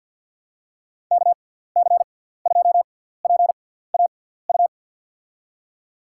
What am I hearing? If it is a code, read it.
Morse code, heard as a keyed tone, about 53 wpm, spaced as for 17 wpm, 700 Hz. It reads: KC2PAU
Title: USA Call Signs - 6 Characters - 6x Character Spacing - 50wpm